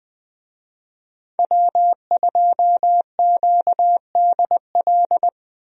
Morse code, heard as a keyed tone, about 20 wpm, 700 Hz. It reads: W2QDL
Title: USA Call Signs - 5 Characters - 20wpm